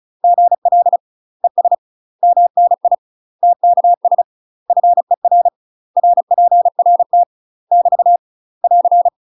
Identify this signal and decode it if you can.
Morse code, heard as a keyed tone, about 36 wpm, 700 Hz. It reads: GL ES MNI TKS FER RPRT <BT> <AR>
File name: UK MASTER QSOs TRAINING / 35WPM / UK MASTER QSOs 0038